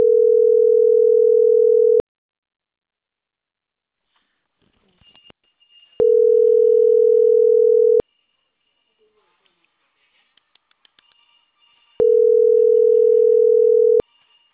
Ringback tone (North America), 440 + 480 Hz.